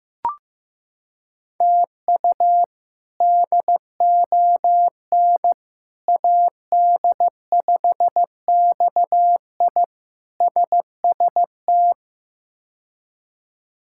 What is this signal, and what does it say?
Morse code, heard as a keyed tone, about 15 wpm, 700 Hz. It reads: TU DON AD5XI SST
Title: K1USN SST QSO Practice - 15wpm